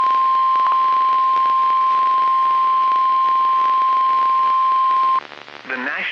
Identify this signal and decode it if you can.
Emergency Alert System (SAME) tones (the 1050 Hz attention tone), with speech. It is heard from 0.7 to 5.2 s.